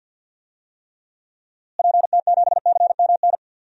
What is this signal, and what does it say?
Morse code, heard as a keyed tone, about 52 wpm, 700 Hz. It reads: PT7CMN
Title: DXCC-100 Call Signs - 6 Characters - Repeated 2x - 50wpm